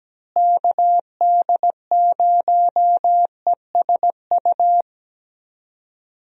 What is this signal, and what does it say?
Morse code, heard as a keyed tone, about 17 wpm, 700 Hz. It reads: KD0ESU
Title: USA Call Signs - 6 Characters - Repeated 3x - 17wpm